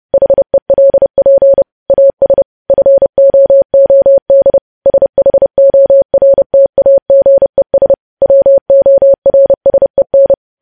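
Morse code, heard as a keyed tone, about 30 wpm, 550 Hz. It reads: HELP AS FOOD SHORTAGES WORSEN
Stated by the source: Morse code sending of news headlines at 30 wpm.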